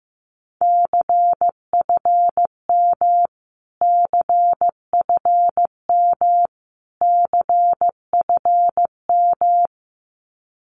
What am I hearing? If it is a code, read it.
Morse code, heard as a keyed tone, about 15 wpm, 700 Hz. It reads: CFM CFM CFM
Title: Abbreviations Repeated 3x - 15wpm